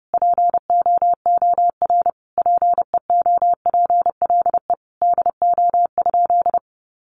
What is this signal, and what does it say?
Morse code, heard as a keyed tone, about 28 wpm, 700 Hz. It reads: POOR PEOPLE DO?
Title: Sentences from Top 1100 Words 30wpm